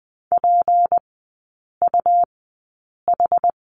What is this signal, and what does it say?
Morse code, heard as a keyed tone, about 20 wpm, spaced as for 8 wpm, 700 Hz. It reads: PUH
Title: 172 Lesson 28e - Number 9 - Three Characters 20wpm